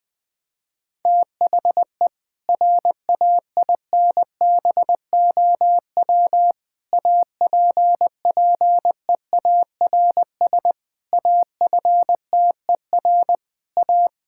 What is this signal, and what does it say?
Morse code, heard as a keyed tone, about 20 wpm, 700 Hz. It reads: THE RAINBOW APPEARS AFTER A
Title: Difficult Sentences with Hints - 20wpm